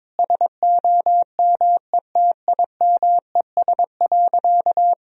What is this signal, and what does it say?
Morse code, heard as a keyed tone, about 21 wpm, 700 Hz. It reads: SOMETIMES.
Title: Sentences from Top 1000 Words - Review Part 3 - 22wpm